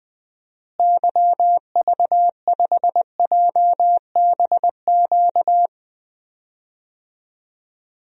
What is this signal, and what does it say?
Morse code, heard as a keyed tone, about 20 wpm, 700 Hz. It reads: YV5JBQ
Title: DXCC-100 Call Signs - 6 Characters - Repeated 2x - 20wpm